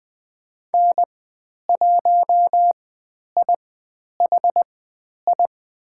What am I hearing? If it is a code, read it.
Morse code, heard as a keyed tone, about 20 wpm, spaced as for 10 wpm, 700 Hz. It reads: N1IHI
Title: 102 Lesson 17g - Letter u - Call Signs 20wpm